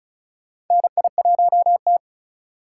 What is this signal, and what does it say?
Morse code, heard as a keyed tone, about 36 wpm, 700 Hz. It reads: NI1T
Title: USA Call Signs - 4 Characters - 3x Character Spacing - 35wpm